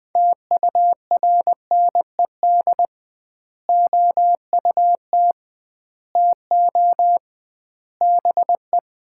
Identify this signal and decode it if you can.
Morse code, heard as a keyed tone, about 20 wpm, 700 Hz. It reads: TURNED OUT TO BE
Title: 2x Word-Spacing - Sets of 5 Words - 20wpm